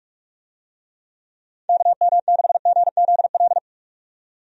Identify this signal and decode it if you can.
Morse code, heard as a keyed tone, about 46 wpm, 700 Hz. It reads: KM6GZL
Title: USA Call Signs - 6 Characters - 45wpm